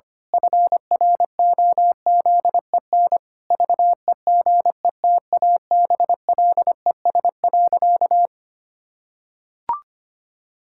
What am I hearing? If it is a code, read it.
Morse code, heard as a keyed tone, about 25 wpm, 700 Hz. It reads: FROZEN VEGETABLES.
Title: Pangram Challenge v2 - 25wpm